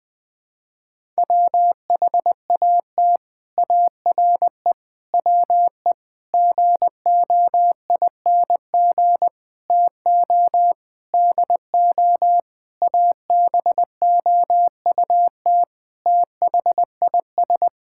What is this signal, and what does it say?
Morse code, heard as a keyed tone, about 20 wpm, 700 Hz. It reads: WHAT ARE WE GOING TO DO ABOUT THIS
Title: Sentences with 8-Word Phrases - 20wpm